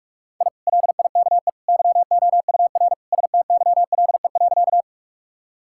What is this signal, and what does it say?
Morse code, heard as a keyed tone, about 43 wpm, 700 Hz. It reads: I LIKE YOUR STYLE.
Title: Sentences from Top 1000 Words - Review Part 5 - 45wpm